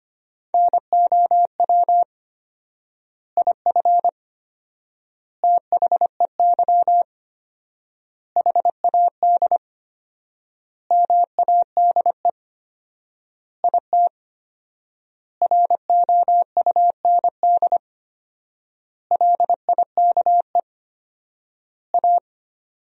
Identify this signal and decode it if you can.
Morse code, heard as a keyed tone, about 25 wpm, 700 Hz. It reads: NOW IF THEY HAD MADE IT ROUND LIKE A
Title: 4x Word-Spacing - Sentences from Top 500 Words - 25wpm